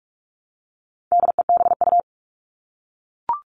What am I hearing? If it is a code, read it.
Morse code, heard as a keyed tone, about 44 wpm, 700 Hz. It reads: DEBU
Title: All Cut Numbers - 4-Digit 45wpm